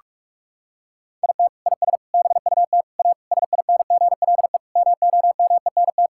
Morse code, heard as a keyed tone, about 46 wpm, 700 Hz. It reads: IT IS BUT A SINGLE MOMENT
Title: Sentences from Top 1000 Words - Review Part 6 - 45wpm